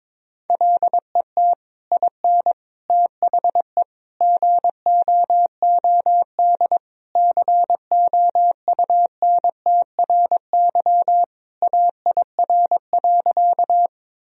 Morse code, heard as a keyed tone, about 22 wpm, 700 Hz. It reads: LET IN THE GOOD COUNTRY AIR.
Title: Sentences from Top 1000 Words - Review Part 1 - 22wpm